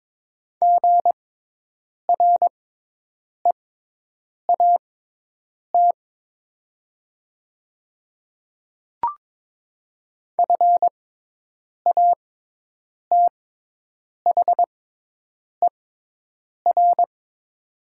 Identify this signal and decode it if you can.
Morse code, heard as a keyed tone, about 22 wpm, spaced as for 8 wpm, 700 Hz. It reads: GREAT FATHER
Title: 6x Character Spacing - Top 200 Words - 22wpm